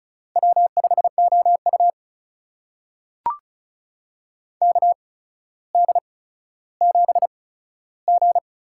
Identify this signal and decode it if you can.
Morse code, heard as a keyed tone, about 35 wpm, spaced as for 15 wpm, 700 Hz. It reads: W5OU KD7G
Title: USA Call Signs - 4 Characters - 8x Character Spacing - 35wpm